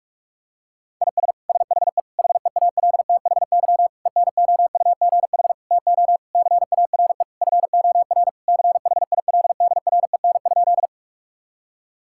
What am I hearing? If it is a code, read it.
Morse code, heard as a keyed tone, about 42 wpm, 700 Hz. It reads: IS SHE HEALTHY ENOUGH TO CARE FOR CHILDREN?
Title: Sentences from Top 1000 Words - Review Part 5 - 45wpm